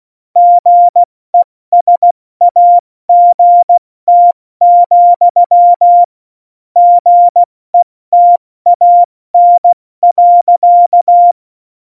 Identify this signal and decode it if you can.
Morse code, heard as a keyed tone, about 16 wpm, spaced as for 14 wpm, 700 Hz. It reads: GESAGT, GETAN.
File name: Morsekurs / Witze / Witz 13. Oktober 2020, 70 F80